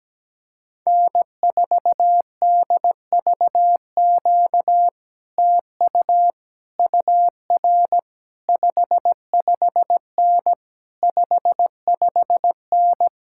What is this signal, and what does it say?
Morse code, heard as a keyed tone, about 17 wpm, 700 Hz. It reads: N4DVQ TU UR 55N 55N
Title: Speed-Racing Common QSO Phrases - 17wpm